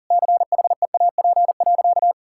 Morse code, heard as a keyed tone, about 40 wpm, 700 Hz. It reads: CHEAP.
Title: Sentences from Top 1600 Words 40wpm